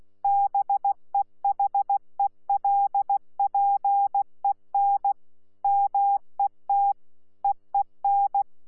Morse code, heard as a keyed tone, about 16 wpm, 800 Hz. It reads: BEHELPEN MET EEN
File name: Z PI4VRZ / PI4VRZ 14-jun-2025